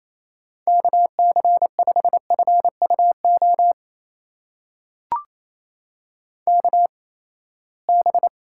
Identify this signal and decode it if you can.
Morse code, heard as a keyed tone, about 28 wpm, spaced as for 19 wpm, 700 Hz. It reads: KC5FUO KB
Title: USA Call Signs - 6 Characters - 8x Character Spacing - 28wpm